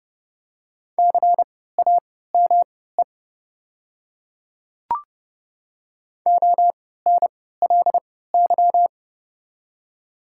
Morse code, heard as a keyed tone, about 29 wpm, spaced as for 17 wpm, 700 Hz. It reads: CAME ONLY
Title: 3x Character Spacing - Top 200 Words - 30wpm